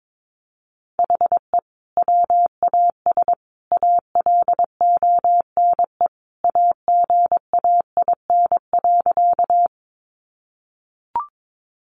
Morse code, heard as a keyed tone, about 22 wpm, 700 Hz. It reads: HE WAS ALONE AGAIN.
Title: Sentences from Top 1000 Words 22wpm